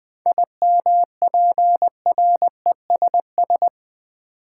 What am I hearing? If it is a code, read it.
Morse code, heard as a keyed tone, about 20 wpm, 700 Hz. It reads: IMPRESS
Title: Seven Letter Words - 20wpm